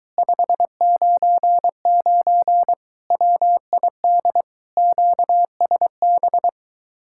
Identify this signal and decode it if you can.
Morse code, heard as a keyed tone, about 23 wpm, 700 Hz. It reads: 599 WID QSB